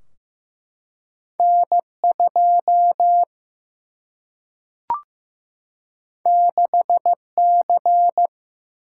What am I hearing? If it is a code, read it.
Morse code, heard as a keyed tone, about 15 wpm, 700 Hz. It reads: N 2 6 C
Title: Single-Letter Single-Number 15wpm